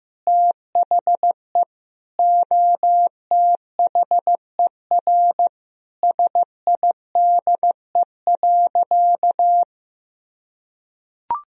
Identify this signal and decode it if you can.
Morse code, heard as a keyed tone, about 15 wpm, 700 Hz. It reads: THE OTHER SIDE.
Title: Sentences from Top 1000 Words - Review Part 3 - 15wpm